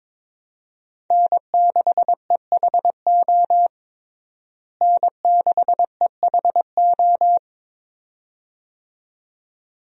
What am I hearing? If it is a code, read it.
Morse code, heard as a keyed tone, about 22 wpm, 700 Hz. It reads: N6EHO N6EHO
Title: USA Call Signs - 5 Characters - Repeated 2x - 22wpm